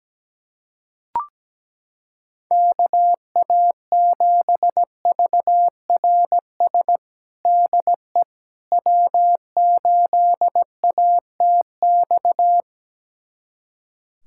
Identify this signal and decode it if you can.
Morse code, heard as a keyed tone, about 17 wpm, 700 Hz. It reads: KA7VRS DE W8ATX
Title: USA Callsign DE Callsign 17wpm